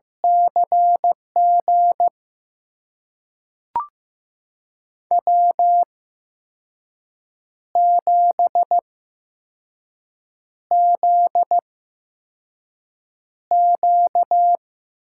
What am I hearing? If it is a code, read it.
Morse code, heard as a keyed tone, about 15 wpm, spaced as for 5 wpm, 700 Hz. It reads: CG W7ZQ
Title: USA Call Signs - 8x Character Spacing - 15wpm